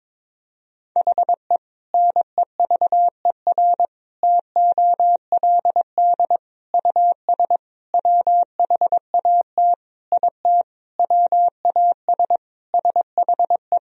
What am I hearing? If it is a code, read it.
Morse code, heard as a keyed tone, about 22 wpm, 700 Hz. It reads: HE NEVER TOLD US WHAT IT WAS SHE
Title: Sentences from Top 400 Words and Abbreviations 22wpm